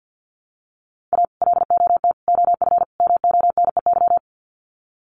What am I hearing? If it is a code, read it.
Morse code, heard as a keyed tone, about 50 wpm, 700 Hz. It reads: A LOT OF MONEY
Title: Sets of 4 Words 50wpm